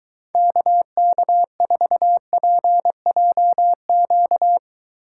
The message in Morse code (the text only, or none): KK4PJQ